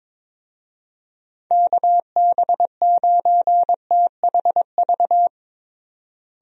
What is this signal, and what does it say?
Morse code, heard as a keyed tone, about 22 wpm, 700 Hz. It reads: KB9THV
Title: USA Call Signs - 6 Characters - Repeated 2x - 22wpm